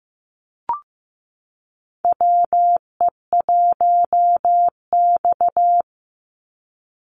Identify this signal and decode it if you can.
Morse code, heard as a keyed tone, about 15 wpm, 700 Hz. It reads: WE1X
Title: USA Call Signs 15wpm